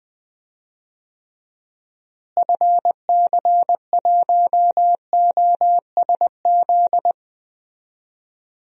Morse code, heard as a keyed tone, about 20 wpm, 700 Hz. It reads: FC1OSZ
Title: DXCC-100 Call Signs - 6 Characters - Repeated 3x - 20wpm